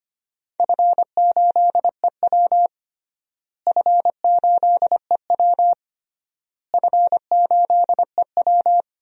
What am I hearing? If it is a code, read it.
Morse code, heard as a keyed tone, about 25 wpm, 700 Hz. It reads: F8EW F8EW F8EW
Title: DXCC-100 Call Signs - 4 Characters - Repeated 3x - 25wpm